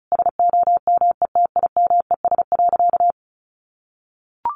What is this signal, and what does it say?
Morse code, heard as a keyed tone, about 34 wpm, 700 Hz. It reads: SOMETIMES.
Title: Sentences from Top 1400 Words 35wpm